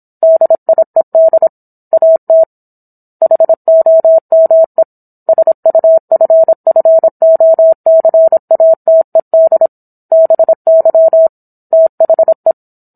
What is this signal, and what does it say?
Morse code, heard as a keyed tone, about 27 wpm, 650 Hz. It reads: DIED AT HOME SUFFOCATED BY THE
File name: Book One / 17-Chapter 17 - 26 WPM